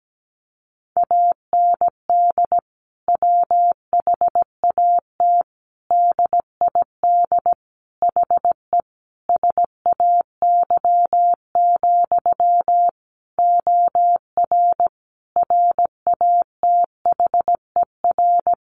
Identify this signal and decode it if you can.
Morse code, heard as a keyed tone, about 17 wpm, 700 Hz. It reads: AND WHAT DID HE SAY, OR RATHER
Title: Sentences from Top 700 Words 17wpm